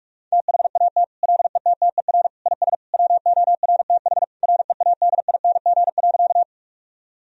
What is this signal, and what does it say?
Morse code, heard as a keyed tone, about 46 wpm, 700 Hz. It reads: THAT LETTER IS WORTH READING.
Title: Sentences from Top 1000 Words - Review Part 3 - 45wpm